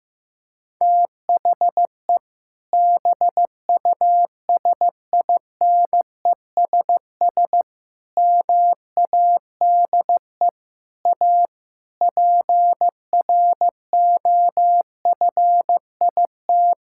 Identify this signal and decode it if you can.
Morse code, heard as a keyed tone, about 16 wpm, 700 Hz. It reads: THE BUSINESS MADE A PROFIT
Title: Sentences from Top 1000 Words - Review Part 5 - 15wpm